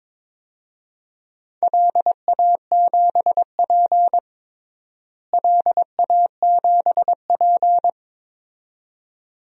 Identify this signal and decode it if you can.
Morse code, heard as a keyed tone, about 22 wpm, 700 Hz. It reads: LA7P LA7P
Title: DXCC-100 Call Signs - 4 Characters - Repeated 2x - 22wpm